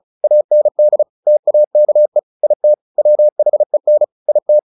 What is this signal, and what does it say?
Morse code, heard as a keyed tone, about 34 wpm, 600 Hz. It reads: AND TAKE IT WHEN IT
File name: the old man and the sea 2 35wpm